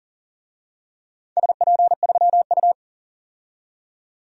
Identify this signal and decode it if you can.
Morse code, heard as a keyed tone, about 37 wpm, 700 Hz. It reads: SP3U
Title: DXCC-100 Call Signs - 4 Characters - 40wpm